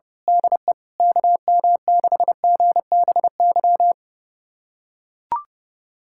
Morse code, heard as a keyed tone, about 30 wpm, 700 Hz. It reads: DE KM6GBY
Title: USA Callsign DE Callsign - 6 Characters - 30wpm